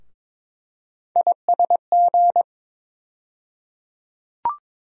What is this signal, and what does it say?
Morse code, heard as a keyed tone, about 22 wpm, 700 Hz. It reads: ISG
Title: USA Call Sign Postfixes - 3 Letters 22wpm